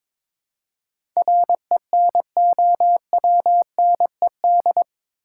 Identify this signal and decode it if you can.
Morse code, heard as a keyed tone, about 21 wpm, 700 Hz. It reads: RENOWNED
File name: practice / 22wpm / Eight Letter Words - 22wpm